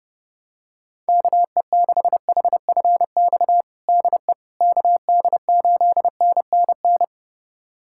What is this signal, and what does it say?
Morse code, heard as a keyed tone, about 29 wpm, 700 Hz. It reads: KE6HFX DE KD8NNN
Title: USA Callsign DE Callsign - 6 Characters - 30wpm